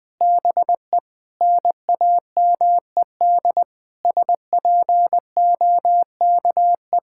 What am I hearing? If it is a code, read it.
Morse code, heard as a keyed tone, about 20 wpm, 700 Hz. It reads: BE NAMED SPOKE
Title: Sentences with 6-Word Phrases - 20wpm